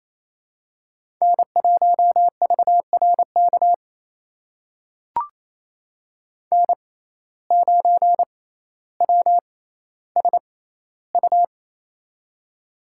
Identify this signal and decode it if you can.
Morse code, heard as a keyed tone, about 28 wpm, spaced as for 14 wpm, 700 Hz. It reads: N1VRK N9WSU
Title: USA Call Signs - 6x Character Spacing - 28wpm